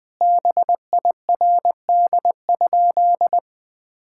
Morse code, heard as a keyed tone, about 21 wpm, 700 Hz. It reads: BIRD?
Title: Sentences with 8-Word Phrases - Spoken Beginning - 20wpm